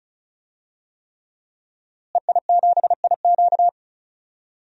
Morse code, heard as a keyed tone, about 36 wpm, 700 Hz. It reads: EI7IQ
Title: DXCC-100 Call Signs - 5 Characters - Repeated 3x - 35wpm